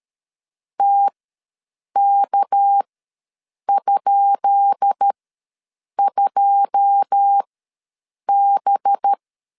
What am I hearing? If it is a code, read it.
Morse code, heard as a keyed tone, about 13 wpm, spaced as for 7 wpm, 800 Hz. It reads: TK?2B
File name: LEZIONI / 10 SIGNS / 10B INT ws 12wpm 3527